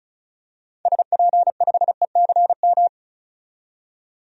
Morse code, heard as a keyed tone, about 36 wpm, 700 Hz. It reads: SP5ECM
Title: DXCC-100 Call Signs - 6 Characters - 35wpm